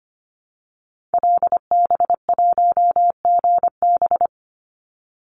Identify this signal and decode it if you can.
Morse code, heard as a keyed tone, about 25 wpm, 700 Hz. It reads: LB1GB
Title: Call Signs Repeated 3x - 25wpm